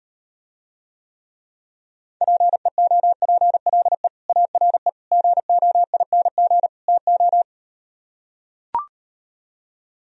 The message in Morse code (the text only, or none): PEOPLE ARE GOING TO